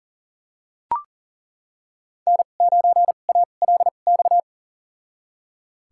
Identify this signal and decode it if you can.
Morse code, heard as a keyed tone, about 38 wpm, spaced as for 28 wpm, 700 Hz. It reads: N9ALX
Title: USA Call Signs - 5 Characters - 2x Character Spacing - 40wpm